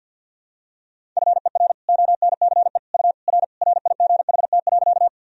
Morse code, heard as a keyed tone, about 53 wpm, 700 Hz. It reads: FER ONCE U R RIGHT.
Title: Sentences from Top 300 Words and Abbreviations 50wpm